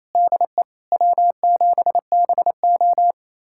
Morse code, heard as a keyed tone, about 28 wpm, 700 Hz. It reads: DE W7BO
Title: USA Callsign DE Callsign - 4 Characters - 28wpm